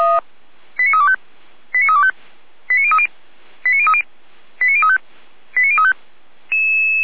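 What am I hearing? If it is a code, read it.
Five-tone selective call (ZVEI), heard as quick runs of five tones, tones of about 70 ms. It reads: 89325 89325 89030 89030 89035 89035
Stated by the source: Five-tone ZVEI selective call 89099, received by SDR for feuerwehr-orlamuende (F85360).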